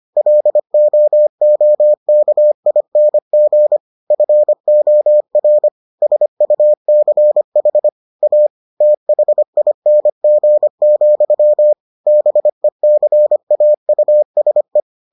Morse code, heard as a keyed tone, about 25 wpm, 600 Hz. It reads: LOOKING FOR SUCH A THING, BECAUSE